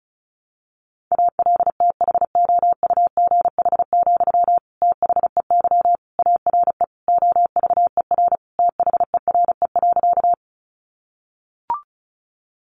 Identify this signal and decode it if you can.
Morse code, heard as a keyed tone, about 34 wpm, 700 Hz. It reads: ALTHOUGH, THEY ARE OVER THERE.
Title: Sentences from Top 600 Words 35wpm